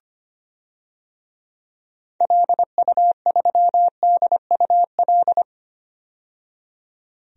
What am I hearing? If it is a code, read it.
Morse code, heard as a keyed tone, about 25 wpm, 700 Hz. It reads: LU3DUL
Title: DXCC-100 Call Signs - 6 Characters - 25wpm